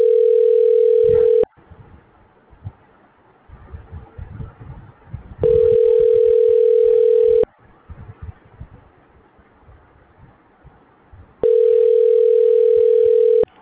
Ringback tone (North America), 440 + 480 Hz.